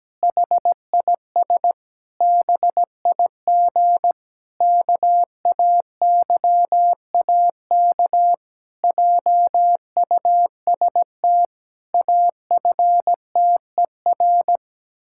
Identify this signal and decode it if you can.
Morse code, heard as a keyed tone, about 17 wpm, 700 Hz. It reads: HIS BIG KAYAK JUST AFTER
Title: Pangram Challenge v2 - 17wpm